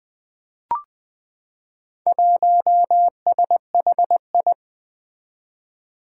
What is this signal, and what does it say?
Morse code, heard as a keyed tone, about 20 wpm, 700 Hz. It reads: 1SHI
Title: Four Challenging Letter-Number Combinations 20wpm